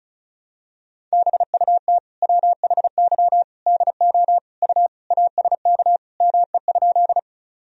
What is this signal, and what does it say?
Morse code, heard as a keyed tone, about 34 wpm, 700 Hz. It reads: BUT WHY DO U ASK ME?
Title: Sentences from Top 200 Words and Abbreviations 35wpm